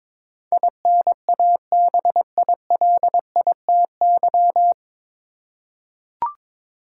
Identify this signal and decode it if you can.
Morse code, heard as a keyed tone, about 22 wpm, 700 Hz. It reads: INABILITY
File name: 9 Letter Words - First 3 Letters Given - 22wpm